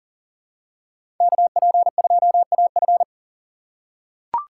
Morse code, heard as a keyed tone, about 40 wpm, 700 Hz. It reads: KP2AF
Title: USA Call Signs - 5 Characters - 2x Character Spacing - 40wpm